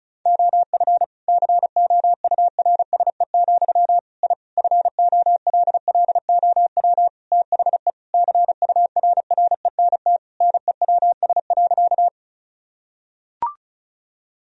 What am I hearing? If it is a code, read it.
Morse code, heard as a keyed tone, about 34 wpm, 700 Hz. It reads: OF COURSE, I FOLLOW THE CURRENT NEWS.
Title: Sentences from Top 1000 Words - Review Part 4 - 35wpm